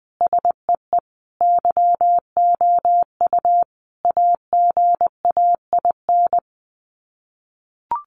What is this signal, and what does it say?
Morse code, heard as a keyed tone, about 20 wpm, 700 Hz. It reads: SEE YOU AGAIN
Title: Sets of 5 Words Encore2 20wpm